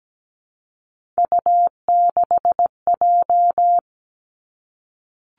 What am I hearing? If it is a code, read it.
Morse code, heard as a keyed tone, about 17 wpm, 700 Hz. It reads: U6J
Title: Three Challenging Letter-Number Combinations 17wpm